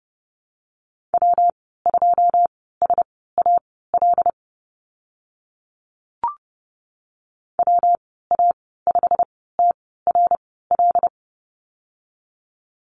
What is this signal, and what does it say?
Morse code, heard as a keyed tone, about 30 wpm, spaced as for 17 wpm, 700 Hz. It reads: W2SAL WA5TRL
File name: Lesson 19g - Letter w - Call Signs 30wpm